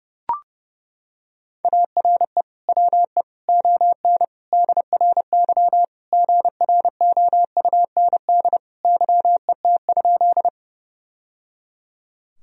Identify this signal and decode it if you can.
Morse code, heard as a keyed tone, about 30 wpm, 700 Hz. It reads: ARE WE ON DRY GROUND YET?
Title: Sentences from Top 1000 Words - Review Part 3 - 30wpm